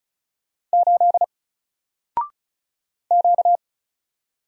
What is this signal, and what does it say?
Morse code, heard as a keyed tone, about 36 wpm, 700 Hz. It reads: 8 Q
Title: Single Character (Letter:Number:Punctuation:Prosign) 35wpm